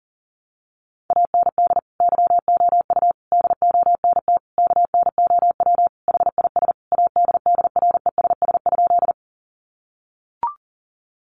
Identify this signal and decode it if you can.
Morse code, heard as a keyed tone, about 40 wpm, 700 Hz. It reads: AND YOU DONT KNOW HIS ADDRESS?
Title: Sentences from Top 1000 Words 40wpm